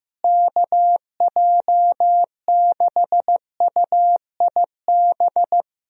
Morse code, heard as a keyed tone, about 15 wpm, 700 Hz. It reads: KJ6UIB
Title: USA Call Signs - 6 Characters - Repeated 3x - 15wpm